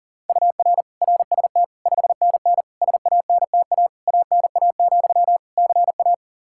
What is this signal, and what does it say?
Morse code, heard as a keyed tone, about 37 wpm, 700 Hz. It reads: UR RST 5NN SANTA ANA, CA
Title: Practice QSO - Exchange 1 - Beginning - 40wpm